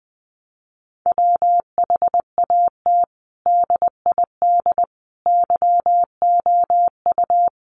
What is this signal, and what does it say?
Morse code, heard as a keyed tone, about 19 wpm, 700 Hz. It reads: WHAT DID YOU
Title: Sentences from Top 1400 Words 20wpm